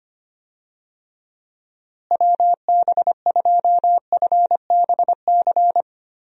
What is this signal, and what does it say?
Morse code, heard as a keyed tone, about 25 wpm, 700 Hz. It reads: WB2FBC
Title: USA Call Signs - 6 Characters - Repeated 3x - 25wpm